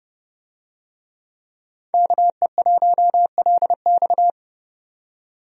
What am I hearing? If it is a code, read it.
Morse code, heard as a keyed tone, about 30 wpm, 700 Hz. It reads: KE1LX
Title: USA Call Signs - 5 Characters - 30wpm